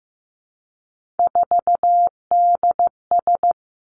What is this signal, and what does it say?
Morse code, heard as a keyed tone, about 15 wpm, 700 Hz. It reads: 4DS
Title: Three Challenging Letter-Number Combinations 15wpm